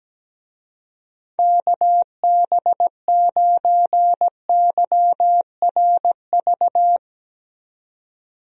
Morse code, heard as a keyed tone, about 17 wpm, 700 Hz. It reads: KB9YRV
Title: USA Call Signs - 6 Characters - 17wpm